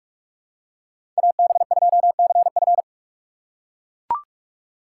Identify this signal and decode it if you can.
Morse code, heard as a keyed tone, about 41 wpm, 690 Hz. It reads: AB2CF
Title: USA Call Signs - 5 Characters - 6x Character Spacing - 45wpm